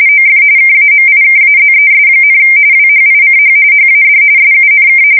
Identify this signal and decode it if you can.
RTTY, about 45.45 baud. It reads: OF KAMIKAZES, WHICH HAD SUNK
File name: fsk170 / battleshipnj